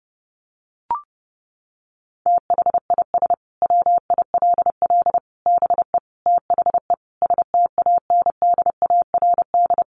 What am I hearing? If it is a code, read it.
Morse code, heard as a keyed tone, about 30 wpm, 700 Hz. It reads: THIS WILL BE THE STANDARD
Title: Sentences from Top 1200 Words 30wpm